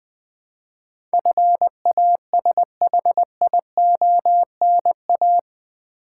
Morse code, heard as a keyed tone, about 20 wpm, 700 Hz. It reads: FASHIONA
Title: Build A Word - 4 Letters At A Time - 11-Letter Words - 20wpm